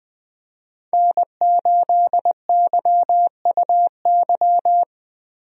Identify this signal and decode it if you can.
Morse code, heard as a keyed tone, about 20 wpm, 700 Hz. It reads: N8YUY
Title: USA Call Signs - 5 Characters - 20wpm